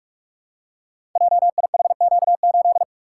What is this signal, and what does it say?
Morse code, heard as a keyed tone, about 44 wpm, 700 Hz. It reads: JIHQ8
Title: Code Groups - 5 Letter:Numbers - 45wpm